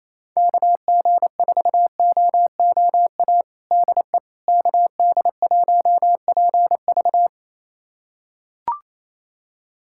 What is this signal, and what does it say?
Morse code, heard as a keyed tone, about 28 wpm, 700 Hz. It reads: KG4OOA DE KD1PV
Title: USA Callsign DE Callsign 28wpm